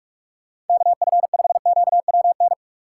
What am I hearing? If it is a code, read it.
Morse code, heard as a keyed tone, about 46 wpm, 700 Hz. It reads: KF5QWN
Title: USA Call Signs - 6 Characters - 2x Character Spacing - 45wpm